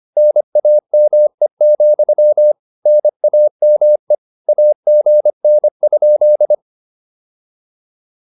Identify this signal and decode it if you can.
Morse code, heard as a keyed tone, about 25 wpm, 600 Hz. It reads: NAME, NAME AGN?